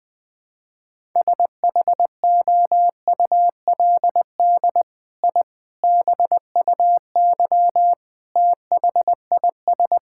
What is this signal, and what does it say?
Morse code, heard as a keyed tone, about 20 wpm, 700 Hz. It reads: SHOULD I BUY THIS
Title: Sentences from Top 1000 Words - Review Part 5 - 20wpm